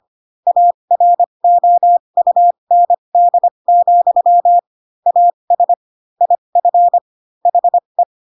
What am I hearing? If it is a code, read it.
Morse code, heard as a keyed tone, about 25 wpm, spaced as for 22 wpm, 700 Hz. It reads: AROUND, AS IF HE